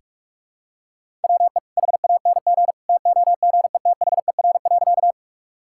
Morse code, heard as a keyed tone, about 42 wpm, 700 Hz. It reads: WE HANG TOGETHER.
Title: Sentences from Top 1600 Words 45wpm